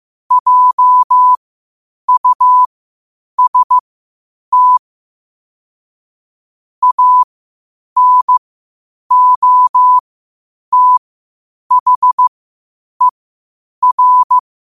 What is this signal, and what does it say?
Morse code, heard as a keyed tone, about 15 wpm, spaced as for 8 wpm, 1000 Hz. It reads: JUST ANOTHER
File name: qotdfiles / 8wpm